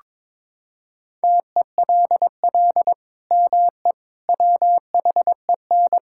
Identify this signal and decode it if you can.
Morse code, heard as a keyed tone, about 22 wpm, 700 Hz. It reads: TELL ME WHEN
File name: practice / 22wpm / Sentences with 2-Word Phrases - 22wpm